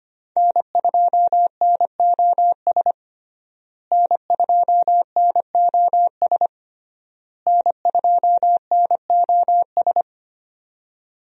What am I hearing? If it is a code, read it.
Morse code, heard as a keyed tone, about 25 wpm, 700 Hz. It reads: N2NOS N2NOS N2NOS
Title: USA Call Signs - 5 Characters - Repeated 3x - 25wpm